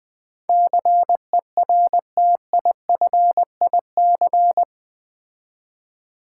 Morse code, heard as a keyed tone, about 20 wpm, 700 Hz. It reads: CERTIFIC
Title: Build A Word - 4 Letters At A Time - 11-Letter Words - 20wpm